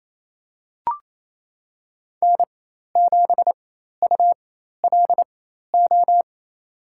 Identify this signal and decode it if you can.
Morse code, heard as a keyed tone, about 28 wpm, spaced as for 13 wpm, 700 Hz. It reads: N7ULO
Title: USA Call Signs - 5 Characters - 4x Character Spacing - 28wpm